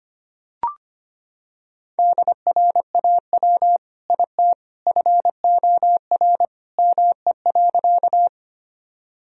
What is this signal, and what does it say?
Morse code, heard as a keyed tone, about 25 wpm, 700 Hz. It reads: DRAW IT FOR ME.
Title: Sentences from Top 1000 Words - Review Part 1 - 25wpm